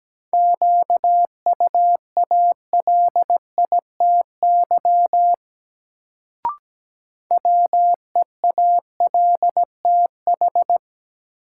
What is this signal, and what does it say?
Morse code, heard as a keyed tone, about 17 wpm, 700 Hz. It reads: QUALITY WEALTH
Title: Sentences from Top 1000 Words - Review Part 4 - New Word Drill - 17wpm